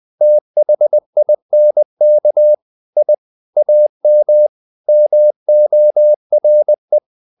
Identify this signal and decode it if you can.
Morse code, heard as a keyed tone, about 20 wpm, 600 Hz. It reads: THINK I AM MORE
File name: the old man and the sea / the old man and the sea part 1 20wpm